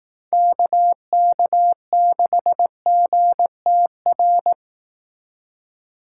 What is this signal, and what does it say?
Morse code, heard as a keyed tone, about 18 wpm, 700 Hz. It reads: KK6GTR
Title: Speed Racing - USA Call Signs 15wpm